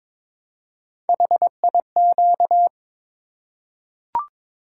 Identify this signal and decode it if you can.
Morse code, heard as a keyed tone, about 22 wpm, 700 Hz. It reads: HIQ
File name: practice / 22wpm / Three Letter-Number Combinations - 22wpm